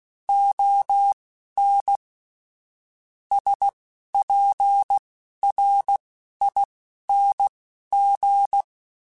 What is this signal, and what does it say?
Morse code, heard as a keyed tone, about 16 wpm, spaced as for 11 wpm, 800 Hz. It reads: ON SPRING